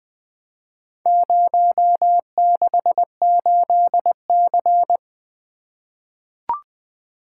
Morse code, heard as a keyed tone, about 21 wpm, 700 Hz. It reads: 068C
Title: Four Letter-Number Combinations - 20wpm